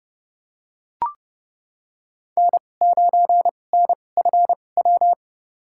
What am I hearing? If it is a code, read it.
Morse code, heard as a keyed tone, about 30 wpm, spaced as for 22 wpm, 700 Hz. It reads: N9NFW
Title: USA Call Signs - 2x Character Spacing - 30wpm